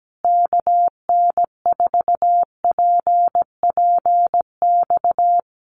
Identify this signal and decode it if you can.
Morse code, heard as a keyed tone, about 17 wpm, 700 Hz. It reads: KN4PPX
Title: Common QSO Phrases Encore 17wpm